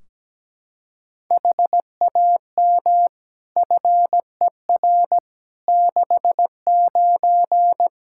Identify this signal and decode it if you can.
Morse code, heard as a keyed tone, about 17 wpm, 700 Hz. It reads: HAM FER 69
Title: Practice QSO - Exchange 3 - More Info - 17wpm